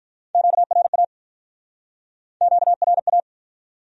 Morse code, heard as a keyed tone, about 47 wpm, 700 Hz. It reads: QRA QRA